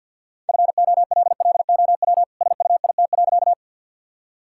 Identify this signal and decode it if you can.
Morse code, heard as a keyed tone, about 50 wpm, 700 Hz. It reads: FOLLOW SUIT.
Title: Sentences from Top 1600 Words 50wpm